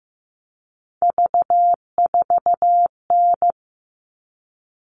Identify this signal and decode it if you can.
Morse code, heard as a keyed tone, about 15 wpm, 700 Hz. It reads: V4N
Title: All Cut Numbers - 3-Digit 15wpm